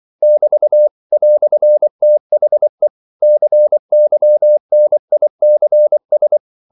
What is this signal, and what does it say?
Morse code, heard as a keyed tone, about 24 wpm, 600 Hz. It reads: <BT> "THE CYNICS
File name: CW / Practice / EnglishQuotations0324